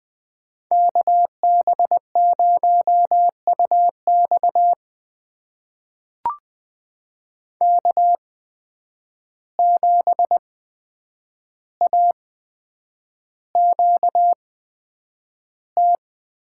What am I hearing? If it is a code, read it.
Morse code, heard as a keyed tone, about 20 wpm, spaced as for 9 wpm, 700 Hz. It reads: KB0UX K7AQT
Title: USA Call Signs - 5 Characters - 8x Character Spacing - 20wpm